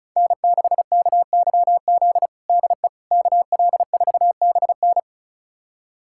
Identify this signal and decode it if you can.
Morse code, heard as a keyed tone, about 36 wpm, 700 Hz. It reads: N6KYZ DE KL4BN
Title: USA Callsign DE Callsign - 5 Characters - 35wpm